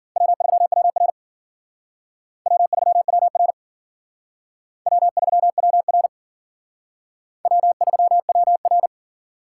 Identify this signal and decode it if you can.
Morse code, heard as a keyed tone, about 52 wpm, 700 Hz. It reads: W3WR W3WR W3WR W3WR